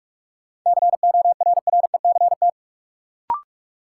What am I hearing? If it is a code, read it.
Morse code, heard as a keyed tone, about 46 wpm, 700 Hz. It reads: CORRECT
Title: Sentences from Top 700 Words - New Word Drill - 45wpm